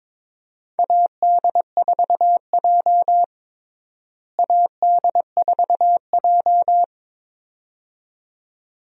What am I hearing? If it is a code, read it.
Morse code, heard as a keyed tone, about 22 wpm, 700 Hz. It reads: AD4J AD4J
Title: USA Call Signs - 4 Characters - Repeated 2x - 22wpm